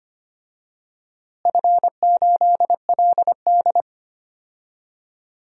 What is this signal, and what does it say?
Morse code, heard as a keyed tone, about 25 wpm, 700 Hz. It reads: F8LD
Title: DXCC-100 Call Signs - 4 Characters - Repeated 2x - 25wpm